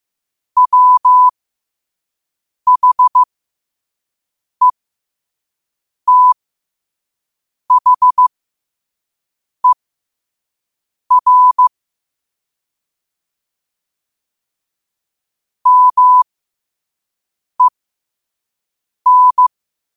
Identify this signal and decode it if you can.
Morse code, heard as a keyed tone, about 15 wpm, spaced as for 5 wpm, 1000 Hz. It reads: WHETHER MEN
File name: qotdfiles / 5wpm